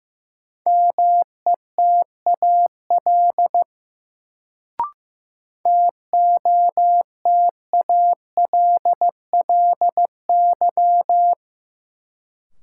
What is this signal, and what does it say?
Morse code, heard as a keyed tone, about 15 wpm, 700 Hz. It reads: METAL TOTALLY
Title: Sentences from Top 1600 Words - New Word Drill Alt - 15wpm